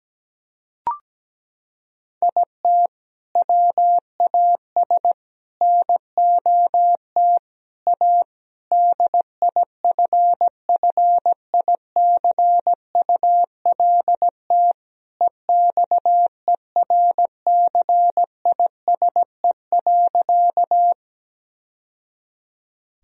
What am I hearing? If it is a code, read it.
Morse code, heard as a keyed tone, about 17 wpm, 700 Hz. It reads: IT WAS NOT A DIFFICULT EXERCISE.
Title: Sentences from Top 1000 Words - Review Part 4 - 17wpm